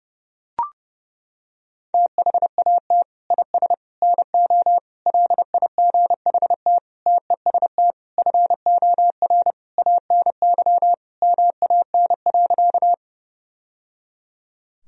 Morse code, heard as a keyed tone, about 30 wpm, 700 Hz. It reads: THAT IS NO LIGHT TEST FOR ANY MAN.
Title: Sentences from Top 1000 Words - Review Part 3 - 30wpm